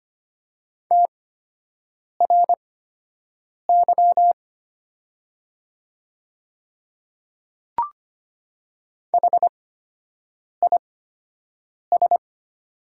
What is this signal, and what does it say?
Morse code, heard as a keyed tone, about 25 wpm, spaced as for 7 wpm, 700 Hz. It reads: TRY HIS
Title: 8x Character Spacing - Top 200 Words - 25wpm